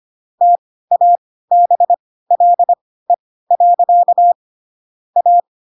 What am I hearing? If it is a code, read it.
Morse code, heard as a keyed tone, about 25 wpm, spaced as for 16 wpm, 700 Hz. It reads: TABLE. A
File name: Pinocchio / Pinocchio ch3 10 25 ik1hge